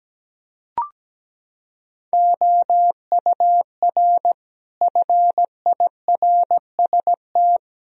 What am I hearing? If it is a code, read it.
Morse code, heard as a keyed tone, about 17 wpm, 700 Hz. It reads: OUR FIRST
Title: Sentences from Top 1000 Words - Review Part 3 - 17wpm